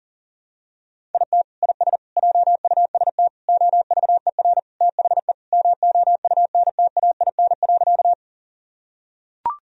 Morse code, heard as a keyed tone, about 40 wpm, 700 Hz. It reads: IT IS JUST OVER THE MOUNTAIN.
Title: Sentences from Top 1000 Words - Review Part 1 - 40wpm